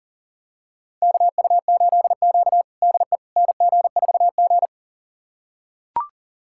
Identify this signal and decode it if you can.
Morse code, heard as a keyed tone, about 40 wpm, 700 Hz. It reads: KU8Q DE NG4G
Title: USA Callsign DE Callsign - 4 Characters - 40wpm